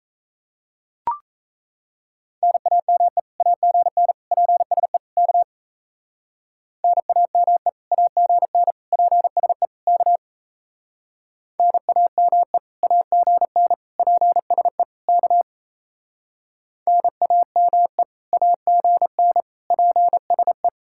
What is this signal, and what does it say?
Morse code, heard as a keyed tone, about 34 wpm, 700 Hz. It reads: NAME AGN PSE K NAME AGN PSE K NAME AGN PSE K NAME AGN PSE
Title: Speed-Racing Common QSO Phrases - 28wpm